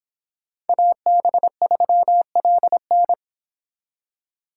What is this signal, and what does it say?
Morse code, heard as a keyed tone, about 26 wpm, 700 Hz. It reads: AB3LN